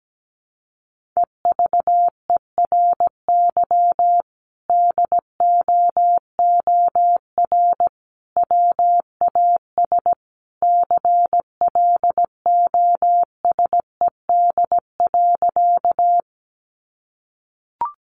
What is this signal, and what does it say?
Morse code, heard as a keyed tone, about 17 wpm, 700 Hz. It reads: EVERY DOOR WAS CLOSED.
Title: Sentences from Top 1400 Words 17wpm